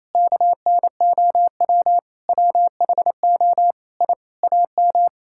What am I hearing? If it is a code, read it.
Morse code, heard as a keyed tone, about 28 wpm, 700 Hz. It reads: KNOW WHO I AM
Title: Sentences from Top 1000 Words - Review Part 5 - 28wpm